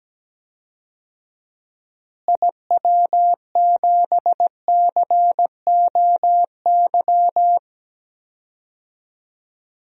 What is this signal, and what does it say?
Morse code, heard as a keyed tone, about 17 wpm, 700 Hz. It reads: IW7COY
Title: DXCC-100 Call Signs - 6 Characters - Repeated 2x - 17wpm